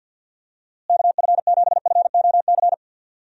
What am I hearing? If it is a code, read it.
Morse code, heard as a keyed tone, about 46 wpm, 700 Hz. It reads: KF7FOC
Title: USA Call Signs - 6 Characters - Repeated 2x - 50wpm